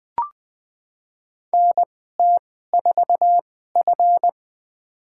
Morse code, heard as a keyed tone, about 20 wpm, spaced as for 14 wpm, 700 Hz. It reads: NT4F
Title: USA Call Signs - 4 Characters - 2x Character Spacing - 20wpm